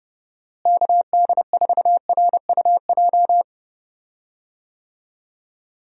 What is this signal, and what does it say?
Morse code, heard as a keyed tone, about 30 wpm, 700 Hz. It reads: KD4RUJ
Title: USA Call Signs - 6 Characters - Repeated 2x - 30wpm